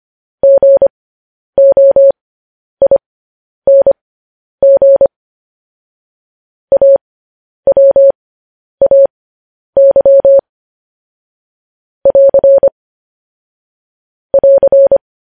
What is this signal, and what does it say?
Morse code, heard as a keyed tone, about 25 wpm, spaced as for 10 wpm, 550 Hz. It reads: GOING AWAY <AR> <AR>
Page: news-headlines-in-morse-code-at-10-wpm